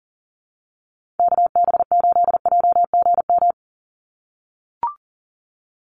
Morse code, heard as a keyed tone, about 40 wpm, 700 Hz. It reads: KB8JGM
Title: USA Call Signs 40wpm